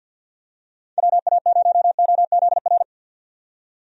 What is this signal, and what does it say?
Morse code, heard as a keyed tone, about 51 wpm, 700 Hz. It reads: WA0OZR
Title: USA Call Signs - 6 Characters - 50wpm